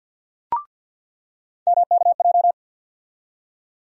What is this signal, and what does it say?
Morse code, heard as a keyed tone, about 50 wpm, 700 Hz. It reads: MKJ
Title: USA Call Sign Postfixes - 3 Letters 50wpm